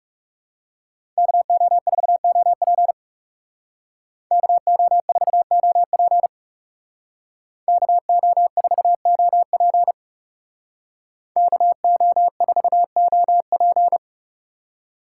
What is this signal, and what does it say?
Morse code, heard as a keyed tone, about 37 wpm, 700 Hz. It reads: KO4OP KO4OP KO4OP KO4OP
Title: Speed Racing - USA 2x2 Call Signs 30wpm